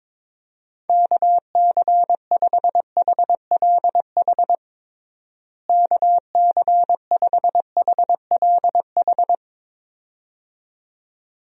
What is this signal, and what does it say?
Morse code, heard as a keyed tone, about 22 wpm, 700 Hz. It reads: KC5HLH KC5HLH
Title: USA Call Signs - 6 Characters - Repeated 2x - 22wpm